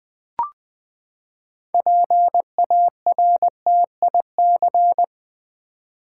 Morse code, heard as a keyed tone, about 20 wpm, 700 Hz. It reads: PARTIC
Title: Build A Word - 6 Letters At A Time - 12-Letter Words - 20wpm